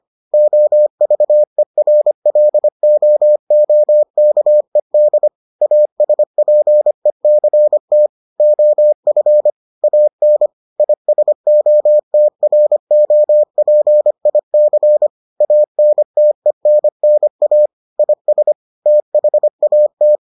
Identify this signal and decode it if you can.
Morse code, heard as a keyed tone, about 25 wpm, 600 Hz. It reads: OVERLOOKED ASPECT OF AN ISOTROPIC ANTENNA IS THAT